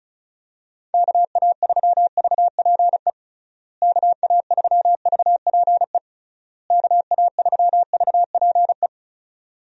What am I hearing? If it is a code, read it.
Morse code, heard as a keyed tone, about 36 wpm, 700 Hz. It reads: KA3VPE KA3VPE KA3VPE
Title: USA Call Signs - 6 Characters - Repeated 3x - 35wpm